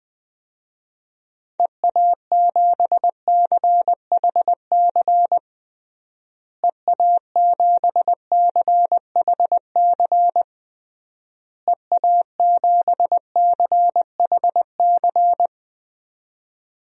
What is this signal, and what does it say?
Morse code, heard as a keyed tone, about 21 wpm, 700 Hz. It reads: EA7CHC EA7CHC EA7CHC
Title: DXCC-100 Call Signs - 6 Characters - Repeated 3x - 20wpm